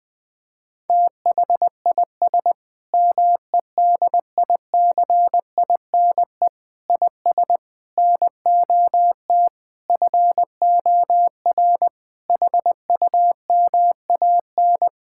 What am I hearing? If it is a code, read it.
Morse code, heard as a keyed tone, about 21 wpm, 700 Hz. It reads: THIS MEDICINE IS NOT FOR HUMAN
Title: Sentences from Top 1000 Words - Review Part 6 - 20wpm